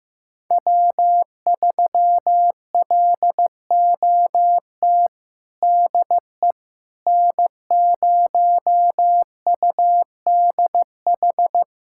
Morse code, heard as a keyed tone, about 15 wpm, 700 Hz. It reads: W3LOT DE N0UDH
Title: USA Callsign DE Callsign - 5 Characters - 15wpm